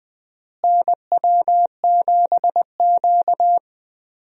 Morse code, heard as a keyed tone, about 20 wpm, 700 Hz. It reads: NW7Q
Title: USA Call Signs - 4 Characters - Repeated 3x - 20wpm